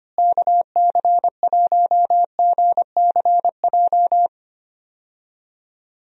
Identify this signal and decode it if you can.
Morse code, heard as a keyed tone, about 25 wpm, 700 Hz. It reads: KC1GCJ
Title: USA Call Signs - 6 Characters - Repeated 3x - 25wpm